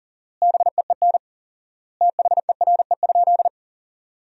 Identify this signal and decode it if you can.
Morse code, heard as a keyed tone, about 40 wpm, 700 Hz. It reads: BEEN THERE?
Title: 4x Word-Spacing - Sentences from Top 300 Words - 40wpm